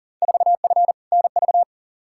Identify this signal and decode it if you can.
Morse code, heard as a keyed tone, about 38 wpm, 700 Hz. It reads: 4F NV
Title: Field Day Exchanges Encore - 40wpm